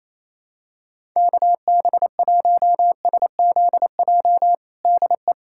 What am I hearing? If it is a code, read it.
Morse code, heard as a keyed tone, about 28 wpm, 700 Hz. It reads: KB1SZJ DE
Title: USA Callsign DE Callsign - 6 Characters - 28wpm